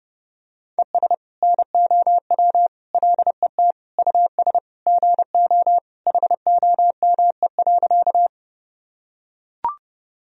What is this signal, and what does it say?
Morse code, heard as a keyed tone, about 31 wpm, 700 Hz. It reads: ES NOW LET US GO HOME.
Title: Sentences from Top 300 Words and Abbreviations 30wpm